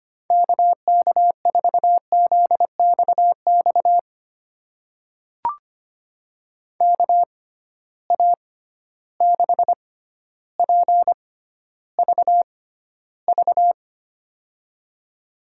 Morse code, heard as a keyed tone, about 25 wpm, spaced as for 13 wpm, 700 Hz. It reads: KK4ZXX KA6PVV
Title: USA Call Signs - 6 Characters - 6x Character Spacing - 25wpm